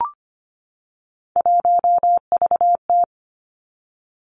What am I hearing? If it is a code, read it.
Morse code, heard as a keyed tone, about 25 wpm, 700 Hz. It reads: 1VT
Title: Three Challenging Letter-Number Combinations 25wpm